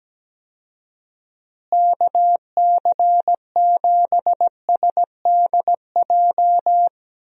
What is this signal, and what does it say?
Morse code, heard as a keyed tone, about 17 wpm, 700 Hz. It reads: KC7SDJ
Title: USA Call Signs - 6 Characters - Repeated 3x - 17wpm